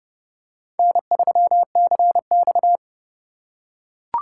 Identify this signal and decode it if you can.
Morse code, heard as a keyed tone, about 29 wpm, 700 Hz. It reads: N3CX